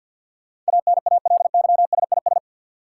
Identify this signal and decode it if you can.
Morse code, heard as a keyed tone, about 49 wpm, 700 Hz. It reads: ANALYSIS
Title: Sentences from Top 1400 Words - New Word Drill Alt - 50wpm